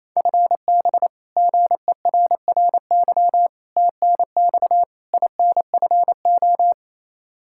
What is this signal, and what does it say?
Morse code, heard as a keyed tone, about 28 wpm, 700 Hz. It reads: FB GERRY TNX INFO
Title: Practice QSO - Exchange 3 - More Info - 28wpm